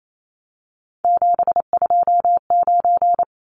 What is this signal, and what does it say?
Morse code, heard as a keyed tone, about 28 wpm, 700 Hz. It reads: 729
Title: Numbers - 3-Digit 28wpm